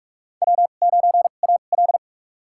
Morse code, heard as a keyed tone, about 46 wpm, spaced as for 32 wpm, 700 Hz. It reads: W9AL
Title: USA Call Signs - 4 Characters - 2x Character Spacing - 45wpm